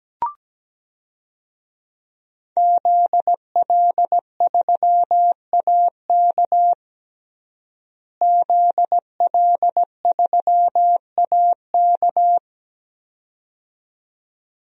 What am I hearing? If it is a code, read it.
Morse code, heard as a keyed tone, about 17 wpm, 700 Hz. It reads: ZL3AK ZL3AK
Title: DXCC-100 Call Signs - 5 Characters - Repeated 2x - 17wpm